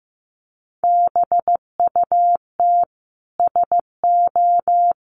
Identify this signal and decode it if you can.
Morse code, heard as a keyed tone, about 15 wpm, 700 Hz. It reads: BUT SO
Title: Sentences from Top 100 Words 15wpm